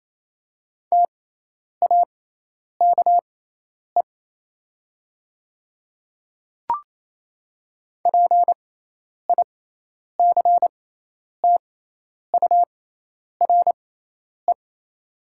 Morse code, heard as a keyed tone, about 28 wpm, spaced as for 10 wpm, 700 Hz. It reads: TAKE PICTURE
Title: 6x Character Spacing - Top 300 Words - 28wpm